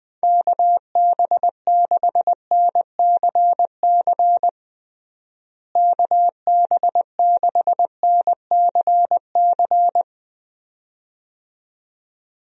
Morse code, heard as a keyed tone, about 21 wpm, 700 Hz. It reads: KB6NCC KB6NCC
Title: USA Call Signs - 6 Characters - Repeated 2x - 20wpm